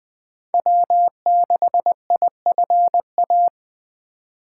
Morse code, heard as a keyed tone, about 24 wpm, 700 Hz. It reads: W6IFA
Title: Speed Racing - USA 1x3 Call Signs 20wpm